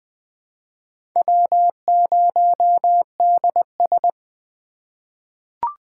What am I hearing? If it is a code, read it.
Morse code, heard as a keyed tone, about 20 wpm, 700 Hz. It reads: W0DS
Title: USA Call Signs - 2x Character Spacing - 20wpm